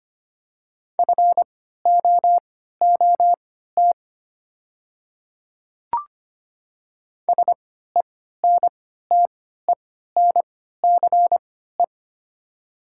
Morse code, heard as a keyed tone, about 25 wpm, spaced as for 14 wpm, 700 Hz. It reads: FOOT SENTENCE
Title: 3x Character Spacing - Top 500 Words - 25wpm